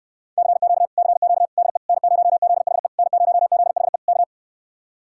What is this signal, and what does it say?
Morse code, heard as a keyed tone, about 69 wpm, 700 Hz. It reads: CQ CQ DE N0ZHE N0ZHE K